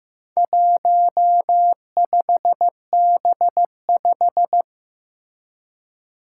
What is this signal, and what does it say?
Morse code, heard as a keyed tone, about 15 wpm, 700 Hz. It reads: 15B5
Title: Four Challenging Letter-Number Combinations 15wpm